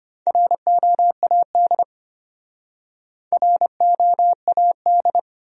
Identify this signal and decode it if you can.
Morse code, heard as a keyed tone, about 31 wpm, 700 Hz. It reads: ROAD ROAD